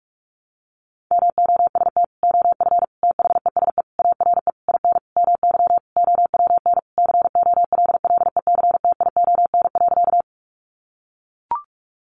Morse code, heard as a keyed tone, about 44 wpm, 700 Hz. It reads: MOST OF THESE ARE IN MY OWN COLLECTION.
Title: Sentences from Top 1200 Words 45wpm